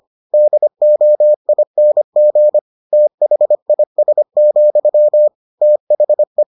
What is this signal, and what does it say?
Morse code, heard as a keyed tone, about 25 wpm, 600 Hz. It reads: DOING THIS, THE